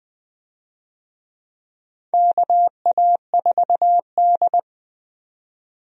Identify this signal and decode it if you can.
Morse code, heard as a keyed tone, about 20 wpm, 700 Hz. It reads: KA4D
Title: USA Call Signs - 4 Characters - 20wpm